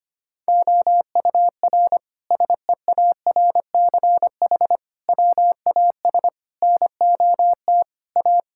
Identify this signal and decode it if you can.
Morse code, heard as a keyed tone, about 25 wpm, 700 Hz. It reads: OUR SEARCH WAS NOT A
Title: Sentences from Top 1000 Words - Review Part 5 - 25wpm